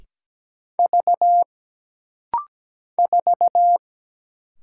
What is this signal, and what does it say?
Morse code, heard as a keyed tone, about 17 wpm, 700 Hz. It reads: V 4
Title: Single Letter-Number - Rapid-Fire 17wpm